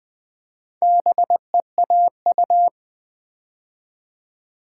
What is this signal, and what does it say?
Morse code, heard as a keyed tone, about 20 wpm, 700 Hz. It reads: BEAU
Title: Build A Word - 2 Letters At A Time - 6-Letter Words - 20wpm